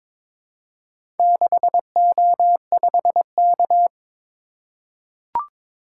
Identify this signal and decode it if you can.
Morse code, heard as a keyed tone, about 22 wpm, 700 Hz. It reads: 6O5K
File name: practice / 22wpm / Four Letter-Number Combinations - 22wpm